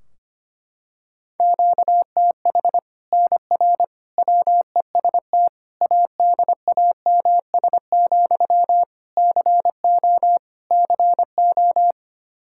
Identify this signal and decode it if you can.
Morse code, heard as a keyed tone, about 25 wpm, 700 Hz. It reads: QTH NR WEST ADAMS, CO CO
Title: Practice QSO - Exchange 1 - Beginning - 25wpm